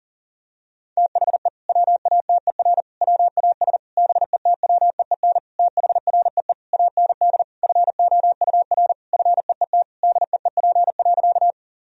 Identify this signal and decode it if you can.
Morse code, heard as a keyed tone, about 40 wpm, 700 Hz. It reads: THE WATER WAS BETWEEN THREE AND FOUR FEET DEEP.
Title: Sentences from Top 1000 Words - Review Part 3 - 40wpm